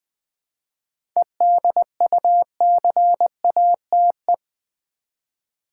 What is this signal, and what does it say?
Morse code, heard as a keyed tone, about 20 wpm, 700 Hz. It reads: EDUCATE
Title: 7 Letter Words - First 3 Letters Given - 20wpm